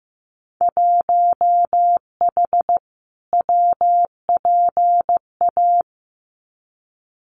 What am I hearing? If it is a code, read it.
Morse code, heard as a keyed tone, about 15 wpm, 700 Hz. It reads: 1H WPA
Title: Winter Field Day Exchanges 15wpm